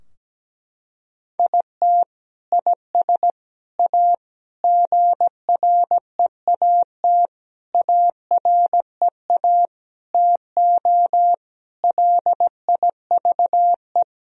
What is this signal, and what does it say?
Morse code, heard as a keyed tone, about 17 wpm, 700 Hz. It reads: IT IS A GREAT AREA TO LIVE
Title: Sentences from Top 1000 Words - Review Part 2 - 17wpm